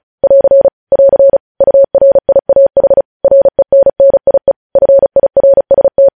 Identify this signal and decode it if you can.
Morse code, heard as a keyed tone, about 36 wpm, 550 Hz. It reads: <AR> <AR> URIAH RENNIE FIRST
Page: news headlines in morse code at 35 wpm